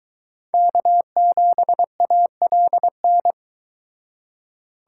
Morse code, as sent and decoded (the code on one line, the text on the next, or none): -.- --... .- .-.. -.
K7ALN